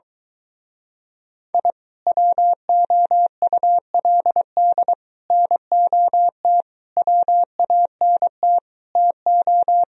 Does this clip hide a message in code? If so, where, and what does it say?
Yes: Morse code from 1.5 to 9.8 s, heard as a keyed tone, about 22 wpm, 700 Hz. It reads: I WOULD NOT WANT TO